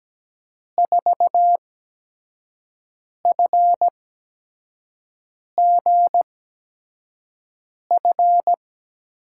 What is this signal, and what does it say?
Morse code, heard as a keyed tone, about 17 wpm, spaced as for 5 wpm, 700 Hz. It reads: 4FGF
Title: USA Call Signs - 6 Characters - 8x Character Spacing - 17wpm